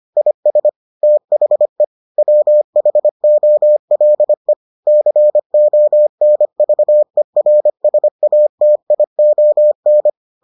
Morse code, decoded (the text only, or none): IS THE WHOLE CONVERSATION